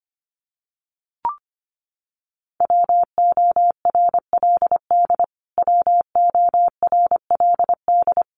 Morse code, heard as a keyed tone, about 25 wpm, 700 Hz. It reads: WORLD WORLD
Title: Top 200 Words Repeated 25wpm